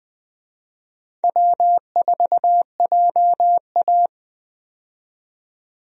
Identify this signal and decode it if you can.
Morse code, heard as a keyed tone, about 20 wpm, 700 Hz. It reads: W4JA
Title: USA Call Signs - 4 Characters - 20wpm